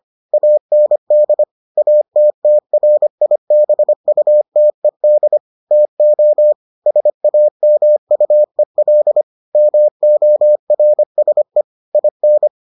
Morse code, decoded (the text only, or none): AND ATTRIBUTED TO SAMUEL MORSE IN